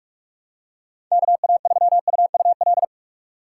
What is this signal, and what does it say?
Morse code, heard as a keyed tone, about 46 wpm, 700 Hz. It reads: KA3UUL
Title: USA Call Signs - 6 Characters - 4x Character Spacing - 45wpm